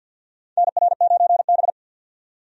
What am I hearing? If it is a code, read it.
Morse code, heard as a keyed tone, about 52 wpm, 700 Hz. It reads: NR9B
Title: USA Call Signs - 4 Characters - 50wpm